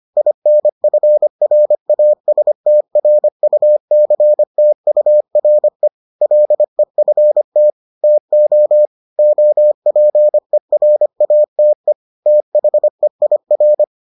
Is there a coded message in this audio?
Yes: Morse code from 0.2 to 13.8 s, heard as a keyed tone, about 25 wpm, 600 Hz. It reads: INFRASTRUCTURE LEFT TO OPERATE THEIR